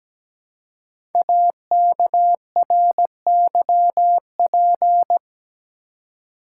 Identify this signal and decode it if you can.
Morse code, heard as a keyed tone, about 17 wpm, 700 Hz. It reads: AKRYP
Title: Code Groups - 5 Letters - 17wpm